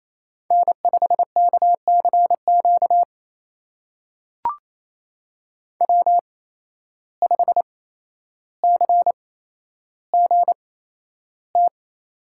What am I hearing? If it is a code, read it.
Morse code, heard as a keyed tone, about 28 wpm, spaced as for 12 wpm, 700 Hz. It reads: N5KCQ W5CGT
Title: USA Call Signs - 5 Characters - 8x Character Spacing - 28wpm